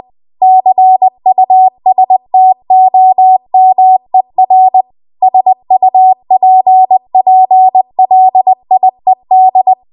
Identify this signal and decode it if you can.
Morse code, heard as a keyed tone, about 20 wpm, 750 Hz. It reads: CUSTOMER SUPPLIED
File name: Archive / 20 WPM / 20WPM